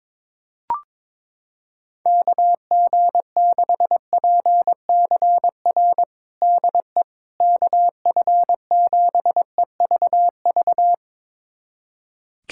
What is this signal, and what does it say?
Morse code, heard as a keyed tone, about 22 wpm, 700 Hz. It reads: KG6PCR DE KF7EVV
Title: USA Callsign DE Callsign - 6 Characters - 22wpm